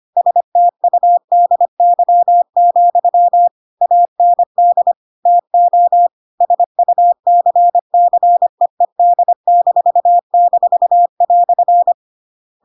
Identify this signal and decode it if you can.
Morse code, heard as a keyed tone, about 25 wpm, 700 Hz. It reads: STUDY, AND TO SUCCEED--"
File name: Pinocchio / Pinocchio ch8 25 25 ik1hge